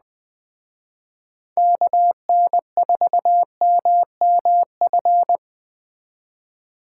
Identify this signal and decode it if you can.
Morse code, heard as a keyed tone, about 20 wpm, 700 Hz. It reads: KN4MMF
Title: Speed Racing - USA 2x3 Call Signs 17wpm